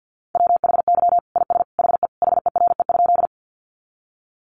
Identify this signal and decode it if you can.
Morse code, heard as a keyed tone, about 50 wpm, 700 Hz. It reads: WHY IS HE HERE?
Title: Sentences from Top 200 Words 50wpm